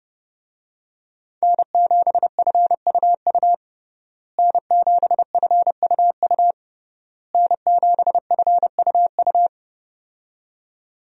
Morse code, heard as a keyed tone, about 31 wpm, 700 Hz. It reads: N7FUU N7FUU N7FUU
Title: USA Call Signs - 5 Characters - Repeated 3x - 30wpm